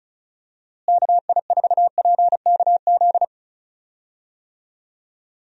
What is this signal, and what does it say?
Morse code, heard as a keyed tone, about 34 wpm, 700 Hz. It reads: KI4PKZ
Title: USA Call Signs - 6 Characters - Repeated 3x - 35wpm